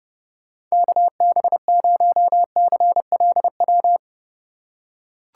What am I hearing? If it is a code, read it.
Morse code, heard as a keyed tone, about 30 wpm, 700 Hz. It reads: KB0CLW
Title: USA Call Signs - 6 Characters - 4x Character Spacing - 30wpm